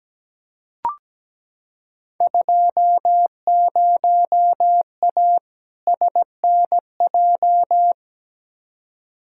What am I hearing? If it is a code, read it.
Morse code, heard as a keyed tone, about 17 wpm, 700 Hz. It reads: 20A SNJ
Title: Field Day Exchanges Encore - 17wpm